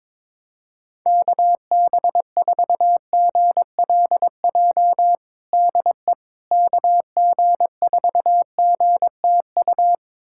Morse code, heard as a keyed tone, about 22 wpm, 700 Hz. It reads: KB4GLJ DE KG4GTU
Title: USA Callsign DE Callsign - 6 Characters - 22wpm